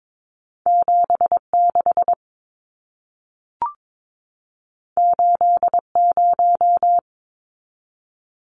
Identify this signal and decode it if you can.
Morse code, heard as a keyed tone, about 22 wpm, 700 Hz. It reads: 76 80
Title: Numbers - 2-Digit 22wpm